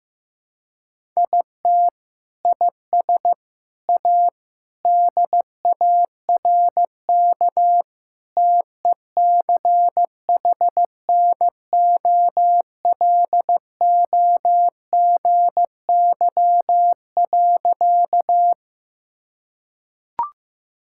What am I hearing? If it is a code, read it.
Morse code, heard as a keyed tone, about 15 wpm, 700 Hz. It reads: IT IS A DARK TECHNOLOGY.
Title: Sentences from Top 1000 Words - Review Part 5 - 15wpm